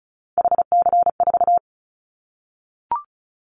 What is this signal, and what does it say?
Morse code, heard as a keyed tone, about 34 wpm, 700 Hz. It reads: HC4
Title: Callsign Prefixes for Top 100 DXCC - 35wpm